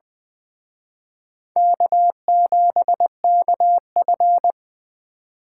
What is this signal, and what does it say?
Morse code, heard as a keyed tone, about 20 wpm, 700 Hz. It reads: K7KF